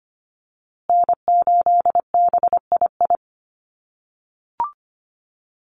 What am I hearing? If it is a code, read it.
Morse code, heard as a keyed tone, about 25 wpm, 700 Hz. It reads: N8BII
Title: USA Call Signs Repeated 2x 25wpm